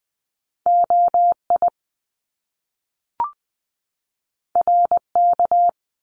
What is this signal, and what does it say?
Morse code, heard as a keyed tone, about 20 wpm, 700 Hz. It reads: OI RK
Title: Top Two Letter Combinations in Words 20wpm